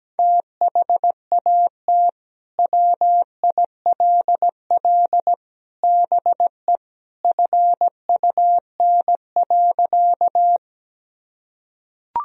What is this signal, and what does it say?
Morse code, heard as a keyed tone, about 17 wpm, 700 Hz. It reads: THAT WILL BE FUN.
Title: Sentences from Top 1000 Words - Review Part 4 - 17wpm